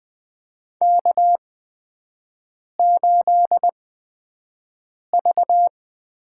Morse code, heard as a keyed tone, about 20 wpm, spaced as for 5 wpm, 700 Hz. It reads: K8V
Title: USA Call Signs - 5 Characters - 8x Character Spacing - 20wpm